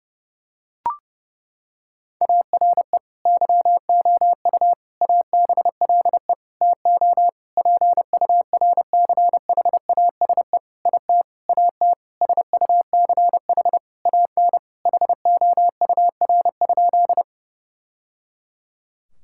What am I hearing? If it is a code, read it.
Morse code, heard as a keyed tone, about 30 wpm, 700 Hz. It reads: ARE YOU ABLE TO PURCHASE IT AT SUCH AN HOUR?
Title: Sentences from Top 1000 Words - Review Part 5 - 30wpm